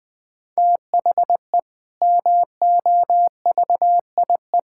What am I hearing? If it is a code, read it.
Morse code, heard as a keyed tone, about 20 wpm, 700 Hz. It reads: THE MOVIE
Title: Sentences with 6-Word Phrases - Spoken Beginning - 20wpm